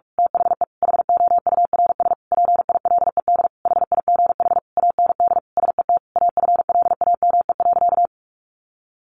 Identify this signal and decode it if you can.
Morse code, heard as a keyed tone, about 44 wpm, 700 Hz. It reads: THE HOURS PILED HIGH AND SET AFLAME.
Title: A Sound Of Thunder 45wpm